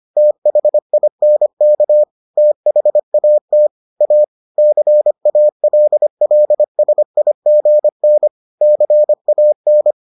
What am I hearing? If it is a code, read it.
Morse code, heard as a keyed tone, about 25 wpm, 600 Hz. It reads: THINK THAT A CALLSIGN CAN